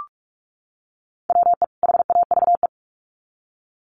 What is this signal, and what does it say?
Morse code, heard as a keyed tone, about 46 wpm, 700 Hz. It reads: WE HAVE
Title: Sets of 2 Words 45wpm